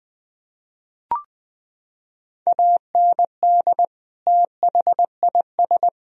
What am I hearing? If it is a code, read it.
Morse code, heard as a keyed tone, about 20 wpm, 700 Hz. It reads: AND THIS
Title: Sentences from Top 1000 Words - Review Part 3 - 20wpm